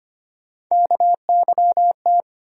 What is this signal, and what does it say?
Morse code, heard as a keyed tone, about 25 wpm, 700 Hz. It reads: KYT
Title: USA Call Sign Postfixes - 3 Letters 25wpm